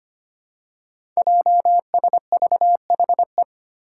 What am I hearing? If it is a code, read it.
Morse code, heard as a keyed tone, about 25 wpm, 700 Hz. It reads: JSVHE
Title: Five Challenging Letter-Number Combinations 25wpm